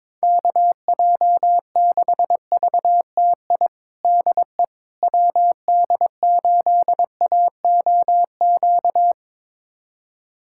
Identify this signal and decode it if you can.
Morse code, heard as a keyed tone, about 22 wpm, 700 Hz. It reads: KJ6VTI DE WD8AOQ
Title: USA Callsign DE Callsign - 6 Characters - 22wpm